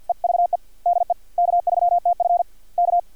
Morse code, heard as a keyed tone, about 50 wpm, 620 Hz. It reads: EVE DE K3TF K
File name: archived / CRSnet